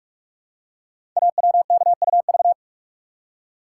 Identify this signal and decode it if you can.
Morse code, heard as a keyed tone, about 44 wpm, 700 Hz. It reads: AWKUV
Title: Code Groups - 5 Letters - 45wpm